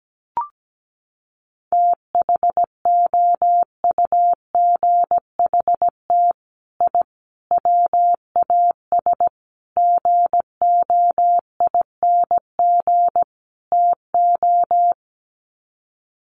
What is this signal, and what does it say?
Morse code, heard as a keyed tone, about 17 wpm, 700 Hz. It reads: THOUGHT I WAS GOING TO
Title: Sets of 5 Words Encore2 17wpm